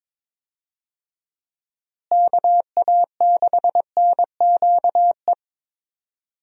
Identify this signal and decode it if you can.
Morse code, heard as a keyed tone, about 22 wpm, 700 Hz. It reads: KA6NQE
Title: USA Call Signs - 6 Characters - Repeated 3x - 22wpm